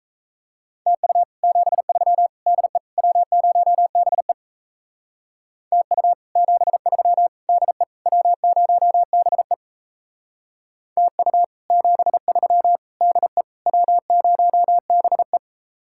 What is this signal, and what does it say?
Morse code, heard as a keyed tone, about 35 wpm, 700 Hz. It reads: TU 73 DE W0BE TU 73 DE W0BE TU 73 DE W0BE